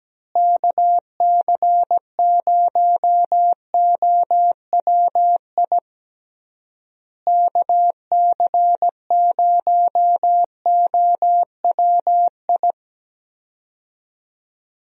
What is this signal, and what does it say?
Morse code, heard as a keyed tone, about 17 wpm, 700 Hz. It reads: KC0OWI KC0OWI
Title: USA Call Signs - 6 Characters - Repeated 3x - 17wpm